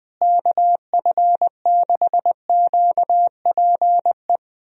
Morse code, heard as a keyed tone, about 20 wpm, 700 Hz. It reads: KF6QPE
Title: USA Call Signs - 6 Characters - Repeated 3x - 20wpm